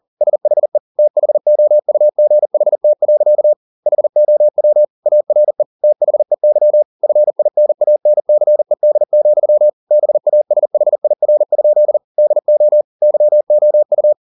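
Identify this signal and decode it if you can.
Morse code, heard as a keyed tone, about 40 wpm, 600 Hz. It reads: SHE THOUGHT. HOW ARE THEY FINANCED, BASHIR? DO YOU
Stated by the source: Morse code at 40 wpm, practice text, CW med Gnister.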